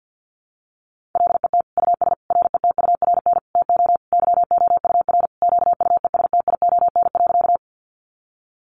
Morse code, heard as a keyed tone, about 50 wpm, 700 Hz. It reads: LET US RETURN TO YOUR QUESTION.
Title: Sentences from Top 800 Words 50wpm